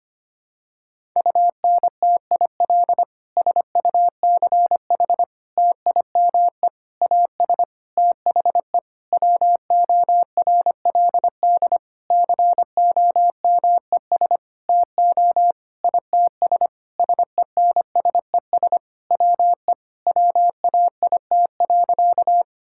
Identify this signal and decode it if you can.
Morse code, heard as a keyed tone, about 25 wpm, 700 Hz. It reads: UNTIL SUCH TIME AS THE WORLD COMES TO ITS SENSES WE WAIT.
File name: practice / 25wpm / Sentences with 9-Word Phrases - 25wpm